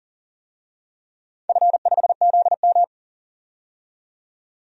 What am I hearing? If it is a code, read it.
Morse code, heard as a keyed tone, about 37 wpm, 700 Hz. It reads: F5ZM
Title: DXCC-100 Call Signs - 4 Characters - 40wpm